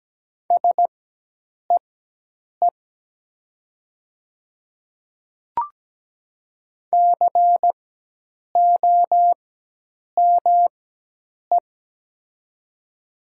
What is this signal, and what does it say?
Morse code, heard as a keyed tone, about 17 wpm, spaced as for 8 wpm, 700 Hz. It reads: SEE COME
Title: 4x Character Spacing - Top 200 Words - 17wpm